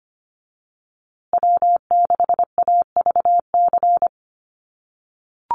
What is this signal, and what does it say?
Morse code, heard as a keyed tone, about 25 wpm, 700 Hz. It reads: W6AVC
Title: USA Call Signs Repeated 3x 25wpm